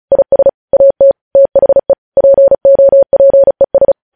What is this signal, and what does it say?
Morse code, heard as a keyed tone, about 36 wpm, 550 Hz. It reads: IS AT THE POPES
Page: News headlines in Morse code at 35 wpm